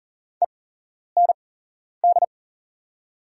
Morse code, heard as a keyed tone, about 39 wpm, spaced as for 11 wpm, 690 Hz. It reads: END
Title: 8x Character Spacing - Top 300 Words - 40wpm